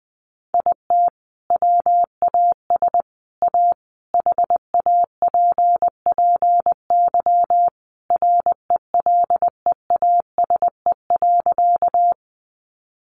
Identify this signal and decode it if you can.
Morse code, heard as a keyed tone, about 20 wpm, 700 Hz. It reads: IT WAS A HAPPY RELEASE.
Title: Sentences from Top 1100 Words 20wpm